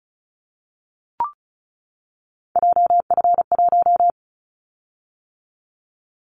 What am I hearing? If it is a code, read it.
Morse code, heard as a keyed tone, about 36 wpm, 700 Hz. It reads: JF1
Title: Callsign Prefixes for Top 25 DXCC - 35wpm